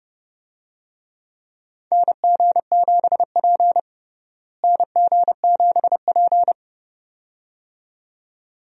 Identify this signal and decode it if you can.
Morse code, heard as a keyed tone, about 32 wpm, 700 Hz. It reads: NG7P NG7P
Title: USA Call Signs - 4 Characters - Repeated 2x - 30wpm